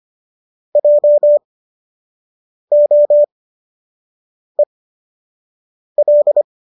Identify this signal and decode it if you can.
Morse code, heard as a keyed tone, about 25 wpm, spaced as for 6 wpm, 590 Hz. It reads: JOEL